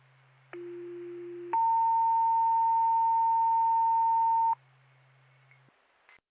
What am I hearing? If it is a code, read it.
Two-tone page, heard as a one-second tone, then a three-second tone, about 350 then 910 Hz. It is heard from 0.5 to 4.5 s.
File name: Blue Ridge Rescue Squad